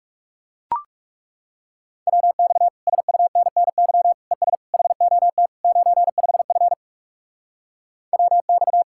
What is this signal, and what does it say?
Morse code, heard as a keyed tone, about 38 wpm, 700 Hz. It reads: WX SUNNY ES HOT 95F WX